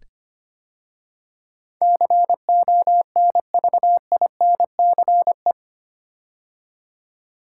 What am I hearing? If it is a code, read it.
Morse code, heard as a keyed tone, about 25 wpm, 700 Hz. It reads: CONVINCE
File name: 8 Letter Words - First 3 Letters Given - 25wpm